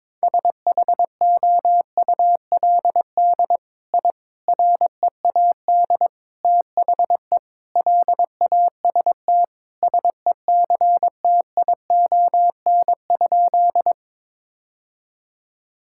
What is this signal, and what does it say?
Morse code, heard as a keyed tone, about 22 wpm, 700 Hz. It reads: SHOULD I READ THE LAST SECTION?
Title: Sentences from Top 1000 Words - Review Part 5 - 22wpm